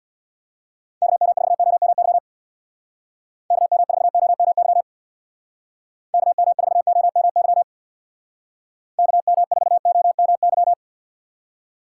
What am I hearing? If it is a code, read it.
Morse code, heard as a keyed tone, about 65 wpm, 700 Hz. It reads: KM4OMY KM4OMY KM4OMY KM4OMY